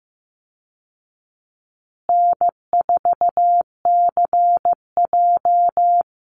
Morse code, heard as a keyed tone, about 15 wpm, 700 Hz. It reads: N4CJ
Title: Easy Call Signs Repeated 3x - 15wpm